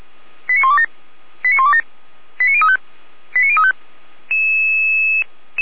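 Five-tone selective call (ZVEI), heard as quick runs of five tones, tones of about 70 ms. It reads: 89127 89127 89035 89035
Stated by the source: ZVEI five-tone selective call 89090 received on an SDR.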